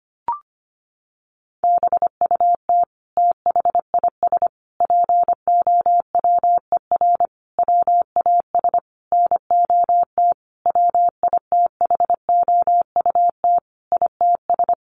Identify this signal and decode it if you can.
Morse code, heard as a keyed tone, about 25 wpm, 700 Hz. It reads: BUT THIS POWER WAS NOT WITHOUT ITS
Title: Sentences from Top 1500 Words 25wpm